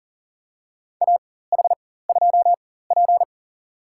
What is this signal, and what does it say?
Morse code, heard as a keyed tone, about 40 wpm, spaced as for 19 wpm, 700 Hz. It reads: AH2P
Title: USA Call Signs - 4 Characters - 4x Character Spacing - 40wpm